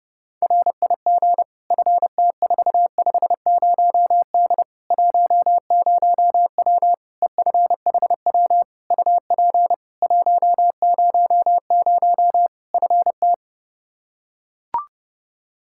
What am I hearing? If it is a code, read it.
Morse code, heard as a keyed tone, about 30 wpm, 700 Hz. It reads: RIG FT450D 10W EFHW UP 100 FT
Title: K1USN SST QSO Practice 30wpm